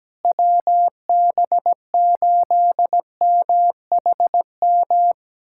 Morse code, heard as a keyed tone, about 17 wpm, 700 Hz. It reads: WB8MHM
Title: USA Callsign DE Callsign - 6 Characters - 17wpm